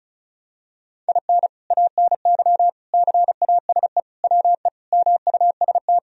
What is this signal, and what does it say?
Morse code, heard as a keyed tone, about 36 wpm, 700 Hz. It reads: IN ANY CASE WE MUST
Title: Sentences from Top 1000 Words - Review Part 1 - 35wpm